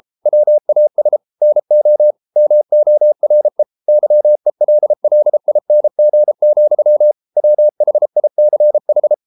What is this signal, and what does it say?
Morse code, heard as a keyed tone, about 33 wpm, 600 Hz. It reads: WAS NO MORE YELLING, WHICH